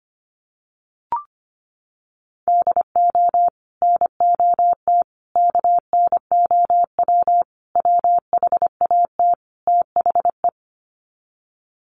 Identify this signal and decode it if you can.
Morse code, heard as a keyed tone, about 25 wpm, 700 Hz. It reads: DO NOT KNOW WHAT THE
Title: Sets of 5 Words 25wpm